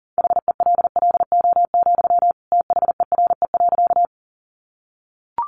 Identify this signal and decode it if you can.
Morse code, heard as a keyed tone, about 40 wpm, 700 Hz. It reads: HELLO, THERE.
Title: Sentences from Top 1100 Words 40wpm